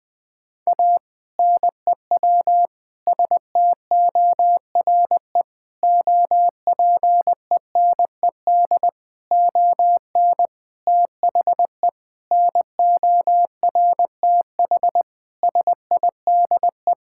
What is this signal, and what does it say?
Morse code, heard as a keyed tone, about 21 wpm, 700 Hz. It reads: A NEW STORE OPENED ON THE NORTH SIDE
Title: Sentences with 6-Word Phrases - 20wpm